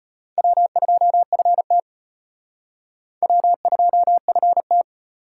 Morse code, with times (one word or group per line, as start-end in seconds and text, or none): 0.4-1.8 W2FT
3.2-4.8 W2FT